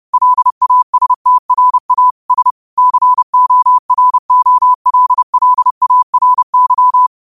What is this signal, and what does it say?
Morse code, heard as a keyed tone, about 30 wpm, 1000 Hz. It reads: LASTRA S COROLLARY